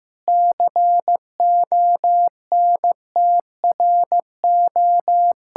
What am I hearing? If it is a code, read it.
Morse code, heard as a keyed tone, about 15 wpm, 700 Hz. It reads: CONTRO
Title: Sentences from Top 1000 Words - Review Part 5 - 15wpm